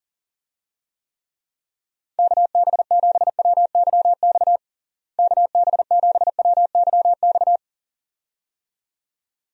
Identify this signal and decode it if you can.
Morse code, heard as a keyed tone, about 40 wpm, 700 Hz. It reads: KB7WYX KB7WYX
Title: USA Call Signs - 6 Characters - Repeated 2x - 40wpm